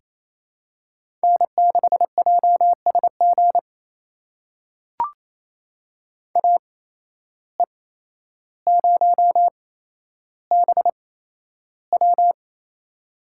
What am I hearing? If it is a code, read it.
Morse code, heard as a keyed tone, about 28 wpm, spaced as for 12 wpm, 700 Hz. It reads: N6JSG AE0BW
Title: USA Call Signs - 5 Characters - 8x Character Spacing - 28wpm